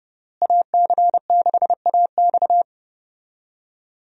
Morse code, heard as a keyed tone, about 30 wpm, 700 Hz. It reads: AC6AX
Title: USA Call Signs - 5 Characters - Repeated 3x - 30wpm